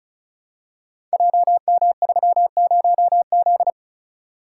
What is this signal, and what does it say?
Morse code, heard as a keyed tone, about 36 wpm, 700 Hz. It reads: JM30Z
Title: Code Groups - 5 Letter:Numbers - 35wpm